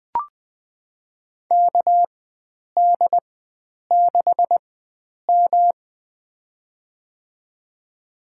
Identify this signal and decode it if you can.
Morse code, heard as a keyed tone, about 20 wpm, spaced as for 9 wpm, 700 Hz. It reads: KD6M
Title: USA Call Signs - 4 Characters - 4x Character Spacing - 20wpm